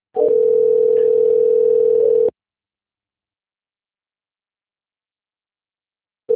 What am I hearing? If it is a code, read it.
Ringback tone (North America), 440 + 480 Hz.